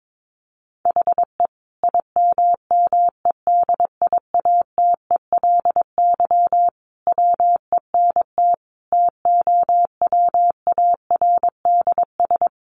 Morse code, heard as a keyed tone, about 22 wpm, 700 Hz. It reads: HE IMMEDIATELY WENT TOWARDS
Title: Sentences from Top 1100 Words 22wpm